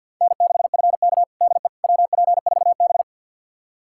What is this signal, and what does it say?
Morse code, heard as a keyed tone, about 51 wpm, 700 Hz. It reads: N6FK DE WP4B
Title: USA Callsign DE Callsign - 4 Characters - 50wpm